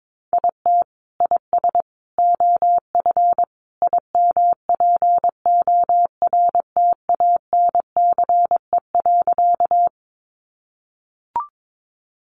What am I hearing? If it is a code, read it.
Morse code, heard as a keyed tone, about 22 wpm, 700 Hz. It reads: IT IS OF IMPORTANCE.
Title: Sentences from Top 1200 Words 22wpm